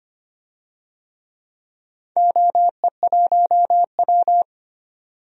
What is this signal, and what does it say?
Morse code, heard as a keyed tone, about 25 wpm, 700 Hz. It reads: OE1W
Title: DXCC-100 Call Signs - 4 Characters - 25wpm